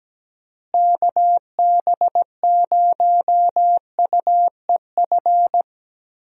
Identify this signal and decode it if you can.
Morse code, heard as a keyed tone, about 17 wpm, 700 Hz. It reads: KB0UEF
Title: USA Call Signs - 6 Characters - Repeated 2x - 17wpm